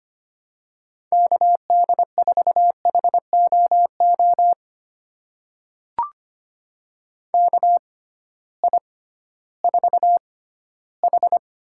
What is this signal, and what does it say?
Morse code, heard as a keyed tone, about 25 wpm, spaced as for 14 wpm, 700 Hz. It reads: KD4HOO KI4H
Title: USA Call Signs - 6 Characters - 6x Character Spacing - 25wpm